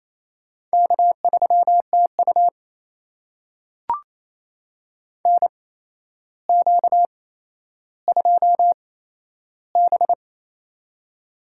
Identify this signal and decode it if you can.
Morse code, heard as a keyed tone, about 28 wpm, spaced as for 12 wpm, 700 Hz. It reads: K3TU NQ2B
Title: USA Call Signs - 8x Character Spacing - 28wpm